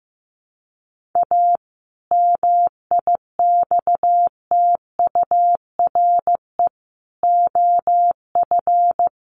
Morse code, heard as a keyed tone, about 15 wpm, 700 Hz. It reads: A MIXTURE OF
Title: Sentences from Top 1500 Words 15wpm